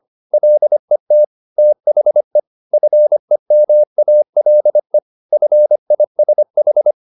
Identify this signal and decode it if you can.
Morse code, heard as a keyed tone, about 25 wpm, 600 Hz. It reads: LET THE FEMALE FISH
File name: the old man and the sea part 1 25wpm